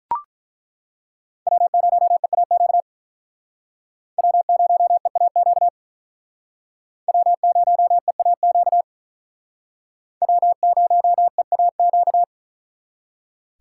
Morse code, heard as a keyed tone, about 42 wpm, 700 Hz. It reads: W0EAQ W0EAQ W0EAQ W0EAQ
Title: Speed Racing - USA 1x3 Call Signs 35wpm